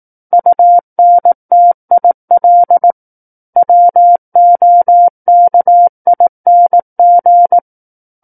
Morse code, heard as a keyed tone, about 18 wpm, 700 Hz. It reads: UNTIL WOKING
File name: Book One / 09-Chapter 9 - 18 WPM